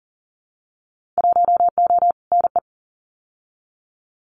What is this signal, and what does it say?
Morse code, heard as a keyed tone, about 41 wpm, 700 Hz. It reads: 1O NE
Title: Winter Field Day Exchanges 40wpm